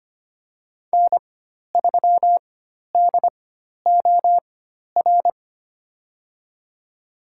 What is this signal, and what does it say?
Morse code, heard as a keyed tone, about 25 wpm, spaced as for 12 wpm, 700 Hz. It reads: N3DOR
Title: USA Call Signs - 5 Characters - 4x Character Spacing - 25wpm